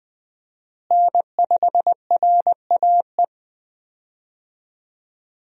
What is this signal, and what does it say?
Morse code, heard as a keyed tone, about 20 wpm, 700 Hz. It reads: N5RAE
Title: USA Call Signs - 5 Characters - Repeated 3x - 20wpm